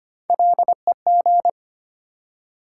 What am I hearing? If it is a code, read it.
Morse code, heard as a keyed tone, about 25 wpm, 700 Hz. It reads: LEG
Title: Three Letter Words - 25wpm